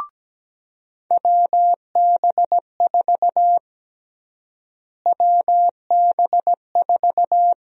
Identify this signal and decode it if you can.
Morse code, heard as a keyed tone, about 17 wpm, 700 Hz. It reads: WB4 WB4
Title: USA Call Sign Prefixes - Repeated 2x - 17wpm